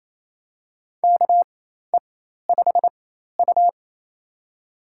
Morse code, heard as a keyed tone, about 28 wpm, spaced as for 13 wpm, 700 Hz. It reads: KE5U
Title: USA Call Signs - 4 Characters - 4x Character Spacing - 28wpm